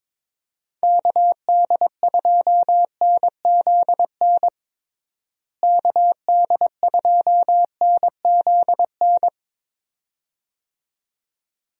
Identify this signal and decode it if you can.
Morse code, heard as a keyed tone, about 22 wpm, 700 Hz. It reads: KD2NZN KD2NZN
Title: USA Call Signs - 6 Characters - Repeated 2x - 22wpm